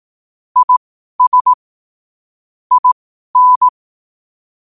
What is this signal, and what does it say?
Morse code, heard as a keyed tone, about 18 wpm, spaced as for 12 wpm, 1000 Hz. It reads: IS IN
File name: qotdfiles / 12wpm